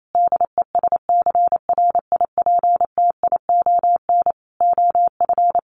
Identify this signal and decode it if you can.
Morse code, heard as a keyed tone, about 28 wpm, 700 Hz. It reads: DESCRIPTION OF
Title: Sentences from Top 1500 Words 28wpm